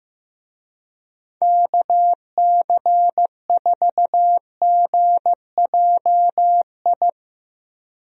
Morse code, heard as a keyed tone, about 15 wpm, 700 Hz. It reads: KC4GJI
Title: USA Call Signs - 6 Characters - Repeated 2x - 15wpm